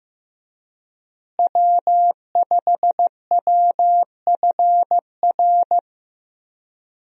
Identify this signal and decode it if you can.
Morse code, heard as a keyed tone, about 15 wpm, 700 Hz. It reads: W5WFR
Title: USA Call Signs - 5 Characters - Repeated 3x - 15wpm